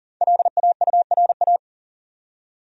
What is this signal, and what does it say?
Morse code, heard as a keyed tone, about 40 wpm, 700 Hz. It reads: LAURA
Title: USA Common Names - Max 5 Letters - 40wpm